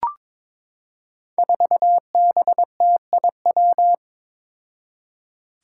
Morse code, heard as a keyed tone, about 21 wpm, 700 Hz. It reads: 4BTIW
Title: Five Challenging Letter-Number Combinations 22wpm